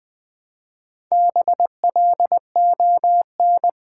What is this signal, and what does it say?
Morse code, heard as a keyed tone, about 20 wpm, 690 Hz. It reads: BLON
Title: Build A Word - 2 Letters At A Time - 6-Letter Words - 20wpm